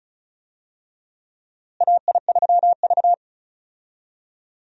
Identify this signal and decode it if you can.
Morse code, heard as a keyed tone, about 36 wpm, 700 Hz. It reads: AI3V
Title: USA Call Signs - 4 Characters - Repeated 2x - 35wpm